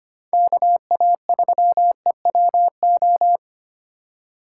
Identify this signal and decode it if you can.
Morse code, heard as a keyed tone, about 28 wpm, 700 Hz. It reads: KA3EWO
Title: Speed Racing - USA 2x3 Call Signs 25wpm